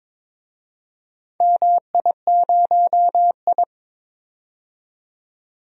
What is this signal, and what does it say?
Morse code, heard as a keyed tone, about 22 wpm, 700 Hz. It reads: MI0I
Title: DXCC-100 Call Signs - 4 Characters - 22wpm